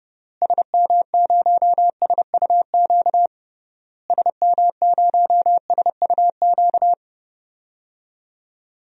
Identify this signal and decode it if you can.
Morse code, heard as a keyed tone, about 32 wpm, 700 Hz. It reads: SM0SUQ SM0SUQ
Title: DXCC-100 Call Signs - 6 Characters - Repeated 3x - 30wpm